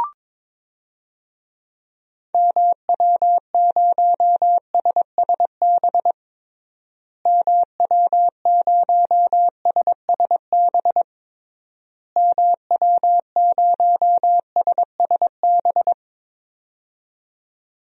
Morse code, heard as a keyed tone, about 22 wpm, 700 Hz. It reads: MW0SSB MW0SSB MW0SSB
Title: DXCC-100 Call Signs - 6 Characters - Repeated 3x - 22wpm